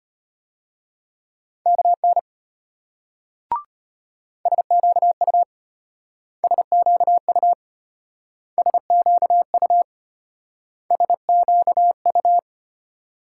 Morse code, heard as a keyed tone, about 32 wpm, 700 Hz. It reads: KN SQU SQU SQU SQU